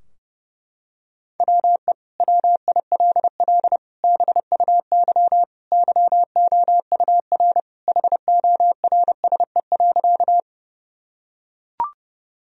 Morse code, heard as a keyed tone, about 30 wpm, 700 Hz. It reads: WE WILL BUY YOUR HORSE.
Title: Sentences from Top 1000 Words - Review Part 4 - 30wpm